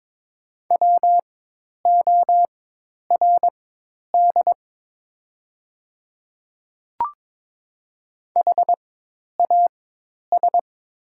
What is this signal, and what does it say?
Morse code, heard as a keyed tone, about 22 wpm, spaced as for 10 wpm, 700 Hz. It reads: WORD HAS
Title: 4x Character Spacing - Top 100 Words - 22wpm